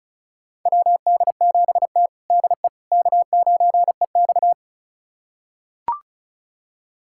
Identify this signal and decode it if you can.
Morse code, heard as a keyed tone, about 36 wpm, 700 Hz. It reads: WD7T DE K9EX
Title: USA Callsign DE Callsign - 4 Characters - 35wpm